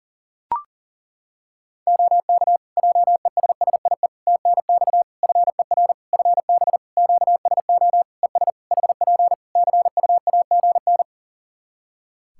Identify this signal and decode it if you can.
Morse code, heard as a keyed tone, about 38 wpm, 700 Hz. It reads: OK JESSIE TNX FER FB QSO ES HP CUAGN
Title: Practice QSO - Exchange 4 - Ending - 40wpm